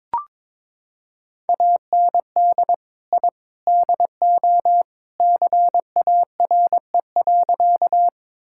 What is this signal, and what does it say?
Morse code, heard as a keyed tone, about 22 wpm, 700 Hz. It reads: AND I DO CARE.
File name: practice / 22wpm / Sentences with 3-Word Phrases - 22wpm